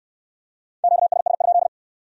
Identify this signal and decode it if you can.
Morse code, heard as a keyed tone, about 62 wpm, 700 Hz. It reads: QSS?